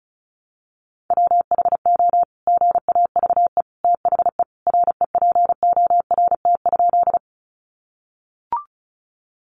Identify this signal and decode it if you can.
Morse code, heard as a keyed tone, about 34 wpm, 700 Hz. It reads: WHO GAVE THE REPORT?
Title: Sentences from Top 900 Words 35wpm